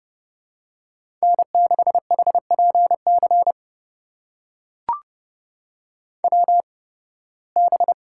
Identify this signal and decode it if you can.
Morse code, heard as a keyed tone, about 30 wpm, spaced as for 20 wpm, 700 Hz. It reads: N6HPC WB
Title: USA Call Signs - 8x Character Spacing - 30wpm